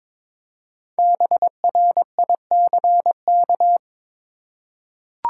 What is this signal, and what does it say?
Morse code, heard as a keyed tone, about 22 wpm, 700 Hz. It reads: BRICK
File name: 5 Letter Words - First 3 Letters Given - 22wpm